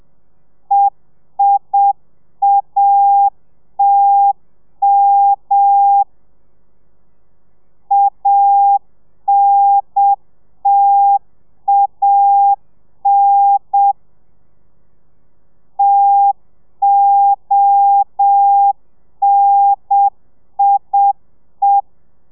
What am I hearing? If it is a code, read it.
Morse code, heard as a keyed tone, about 7 wpm, 800 Hz. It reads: EIATM ANTAN TONIE